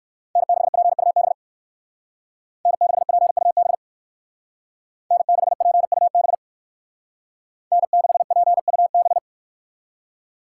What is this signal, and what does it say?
Morse code, heard as a keyed tone, about 58 wpm, 700 Hz. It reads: N6PUB N6PUB N6PUB N6PUB